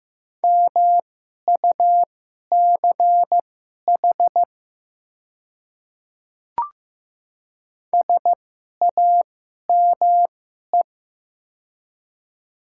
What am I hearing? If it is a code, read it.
Morse code, heard as a keyed tone, about 15 wpm, spaced as for 11 wpm, 700 Hz. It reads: MUCH SAME
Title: 2x Character Spacing - Top 300 Words - 15wpm